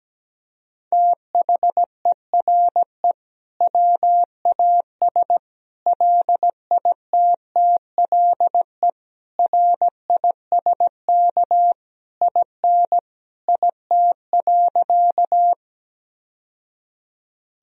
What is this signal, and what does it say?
Morse code, heard as a keyed tone, about 17 wpm, 700 Hz. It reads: THERE WAS LITTLE RISK IN IT.
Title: Sentences from Top 1000 Words - Review Part 4 - 17wpm